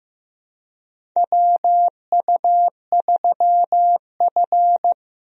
Morse code, heard as a keyed tone, about 16 wpm, 700 Hz. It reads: WU3F
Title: USA Call Signs - 4 Characters - Repeated 3x - 15wpm